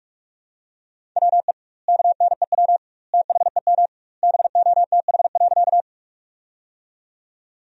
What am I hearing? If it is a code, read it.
Morse code, heard as a keyed tone, about 46 wpm, 700 Hz. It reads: WE KNEW THEM BOTH.
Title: 2x Word-Spacing - Sentences from Top 400 Words - 45wpm